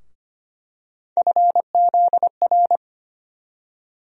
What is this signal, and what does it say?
Morse code, heard as a keyed tone, about 25 wpm, 700 Hz. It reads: FZR
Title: USA Call Sign Postfixes - 3 Letters 25wpm